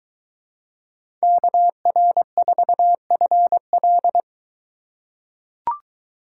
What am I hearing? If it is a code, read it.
Morse code, heard as a keyed tone, about 21 wpm, 700 Hz. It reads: KR4FL